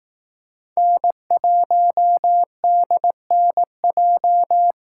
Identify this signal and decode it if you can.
Morse code, heard as a keyed tone, about 18 wpm, 700 Hz. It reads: N1DNJ
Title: Speed Racing - USA Call Signs 15wpm